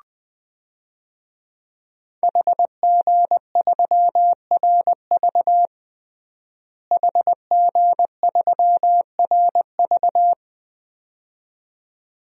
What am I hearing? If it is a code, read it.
Morse code, heard as a keyed tone, about 20 wpm, 700 Hz. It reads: HG3RV HG3RV
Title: DXCC-100 Call Signs - 5 Characters - Repeated 2x - 20wpm